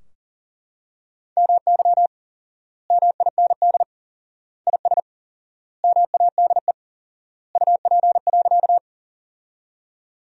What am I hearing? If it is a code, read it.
Morse code, heard as a keyed tone, about 40 wpm, 700 Hz. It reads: MY MIND IS MADE UP.
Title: 4x Word-Spacing - Sentences from Top 500 Words - 40wpm